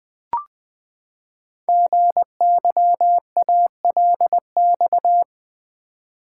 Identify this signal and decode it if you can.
Morse code, heard as a keyed tone, about 20 wpm, 700 Hz. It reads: GYALX
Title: Code Groups - 5 Letters - 20wpm